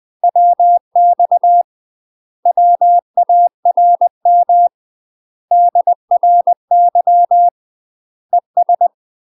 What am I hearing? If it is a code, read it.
Morse code, heard as a keyed tone, about 20 wpm, 700 Hz. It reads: WX WARM DRY ES
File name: UK MASTER QSOs TRAINING/20WPM